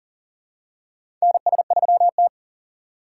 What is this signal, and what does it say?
Morse code, heard as a keyed tone, about 38 wpm, 700 Hz. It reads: NS3T
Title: USA Call Signs - 3x Character Spacing - 40wpm